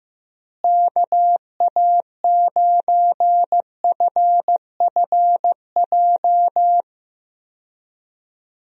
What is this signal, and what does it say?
Morse code, heard as a keyed tone, about 16 wpm, 700 Hz. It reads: KA9FFJ
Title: USA Call Signs - 6 Characters - Repeated 3x - 15wpm